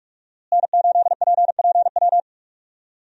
Morse code, heard as a keyed tone, about 46 wpm, 700 Hz. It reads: N8PPW
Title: USA Call Signs - 5 Characters - Repeated 3x - 45wpm